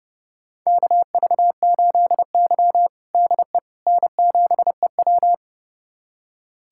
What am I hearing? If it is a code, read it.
Morse code, heard as a keyed tone, about 30 wpm, 700 Hz. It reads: KV8Y DE N7EW
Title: USA Callsign DE Callsign - 4 Characters - 30wpm